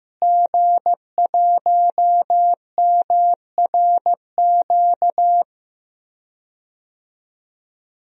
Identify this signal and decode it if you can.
Morse code, heard as a keyed tone, about 15 wpm, 700 Hz. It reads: G1MRQ
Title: DXCC-100 Call Signs - 5 Characters - Repeated 2x - 15wpm